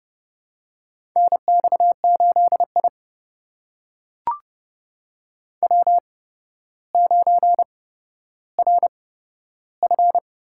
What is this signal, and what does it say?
Morse code, heard as a keyed tone, about 28 wpm, spaced as for 13 wpm, 700 Hz. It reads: NX8I W9RF
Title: USA Call Signs - 4 Characters - 8x Character Spacing - 30wpm